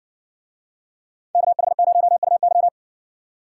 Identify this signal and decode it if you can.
Morse code, heard as a keyed tone, about 56 wpm, 700 Hz. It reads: KH0UY